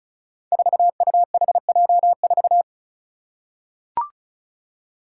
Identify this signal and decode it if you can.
Morse code, heard as a keyed tone, about 36 wpm, 700 Hz. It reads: 4UHJ4
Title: Five Challenging Letter-Number Combinations 35wpm